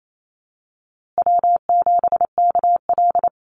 Morse code, heard as a keyed tone, about 28 wpm, 700 Hz. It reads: W7KL
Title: USA Call Signs Repeated 2x 28wpm